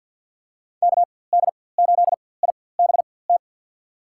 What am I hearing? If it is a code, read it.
Morse code, heard as a keyed tone, about 48 wpm, spaced as for 23 wpm, 700 Hz. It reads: KD8IBT
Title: USA Call Signs - 6 Characters - 4x Character Spacing - 50wpm